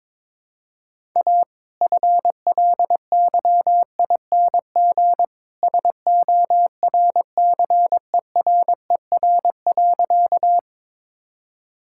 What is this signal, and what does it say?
Morse code, heard as a keyed tone, about 22 wpm, 690 Hz. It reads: A FLYING SORCERER.
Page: morse-code-ninja-practice-22wpm